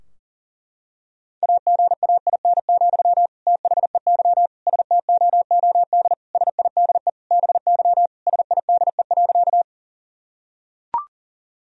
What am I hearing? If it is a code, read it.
Morse code, heard as a keyed tone, about 40 wpm, 700 Hz. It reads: AGAIN, THEY STOOD SIDE BY SIDE.
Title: Sentences from Top 1000 Words - Review Part 2 - 40wpm